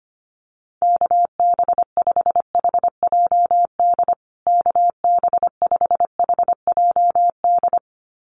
Morse code, heard as a keyed tone, about 25 wpm, 700 Hz. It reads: KB5HJD KB5HJD
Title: USA Call Signs Repeated 2x 25wpm